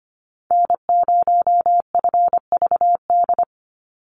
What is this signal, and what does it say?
Morse code, heard as a keyed tone, about 25 wpm, 700 Hz. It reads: N0FVD
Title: USA Call Signs 25wpm